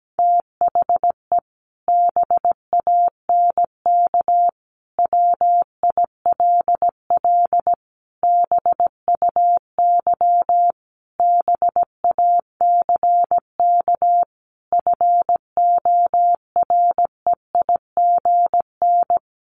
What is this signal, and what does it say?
Morse code, heard as a keyed tone, about 17 wpm, 700 Hz. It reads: THE BANK WILL BUY BACK FOREIGN
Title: Sentences from Top 1500 Words 17wpm